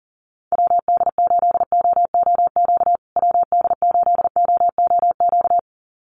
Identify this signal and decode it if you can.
Morse code, heard as a keyed tone, about 40 wpm, 700 Hz. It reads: WD8OOQ WD8OOQ
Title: USA Call Signs Repeated 2x 40wpm